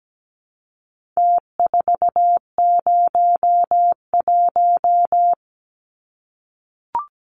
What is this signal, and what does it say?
Morse code, heard as a keyed tone, about 17 wpm, 700 Hz. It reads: T401
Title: Numbers - Contest Style - 4-Digit 17wpm